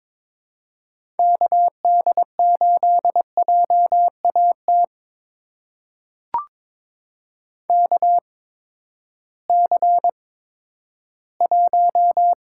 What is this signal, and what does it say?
Morse code, heard as a keyed tone, about 22 wpm, spaced as for 12 wpm, 700 Hz. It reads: KD8JAT KC1
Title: USA Call Signs - 6 Characters - 8x Character Spacing - 22wpm